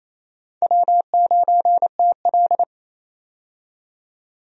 Morse code, heard as a keyed tone, about 28 wpm, 700 Hz. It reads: W9TL
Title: USA Call Signs - 4 Characters - 28wpm